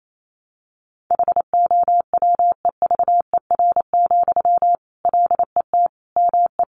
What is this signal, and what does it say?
Morse code, heard as a keyed tone, about 28 wpm, 700 Hz. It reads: HOWEVER, LET ME
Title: Sentences from Top 600 Words 28wpm